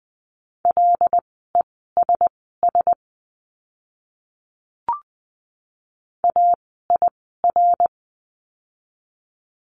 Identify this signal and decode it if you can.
Morse code, heard as a keyed tone, about 20 wpm, spaced as for 14 wpm, 700 Hz. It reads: LESS AIR
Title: 075 Lesson 13f - Letter L - Words 20wpm